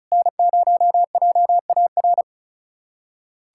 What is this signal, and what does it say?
Morse code, heard as a keyed tone, about 36 wpm, 700 Hz. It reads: N0JAR
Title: USA Callsign DE Callsign - 5 Characters - 35wpm